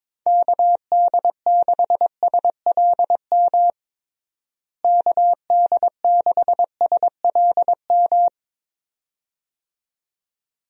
Morse code, heard as a keyed tone, about 22 wpm, 700 Hz. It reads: KD6SLM KD6SLM
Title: USA Call Signs - 6 Characters - Repeated 2x - 22wpm